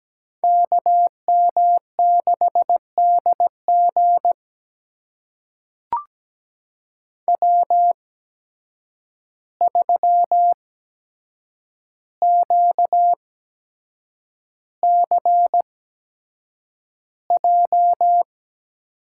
Morse code, heard as a keyed tone, about 17 wpm, spaced as for 7 wpm, 700 Hz. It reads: KM6DG W3QCJ
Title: USA Call Signs - 8x Character Spacing - 17wpm